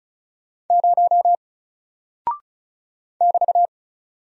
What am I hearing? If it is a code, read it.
Morse code, heard as a keyed tone, about 36 wpm, 700 Hz. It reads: 0 <BT>
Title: Single Character (Letter:Number:Punctuation:Prosign) Alt 35wpm